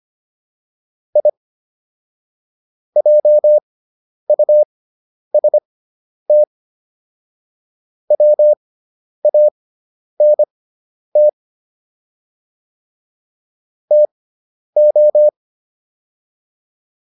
Morse code, heard as a keyed tone, about 25 wpm, spaced as for 10 wpm, 600 Hz. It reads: I JUST WANT TO